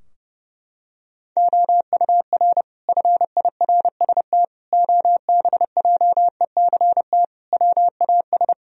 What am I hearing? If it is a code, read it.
Morse code, heard as a keyed tone, about 32 wpm, 700 Hz. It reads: OUR FIRST OBJECT WAS
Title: Sentences from Top 1000 Words - Review Part 3 - 30wpm